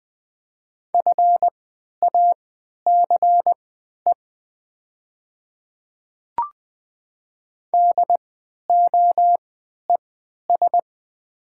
Morse code, heard as a keyed tone, about 20 wpm, spaced as for 11 wpm, 700 Hz. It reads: FACE DOES
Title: 3x Character Spacing - Top 300 Words - 20wpm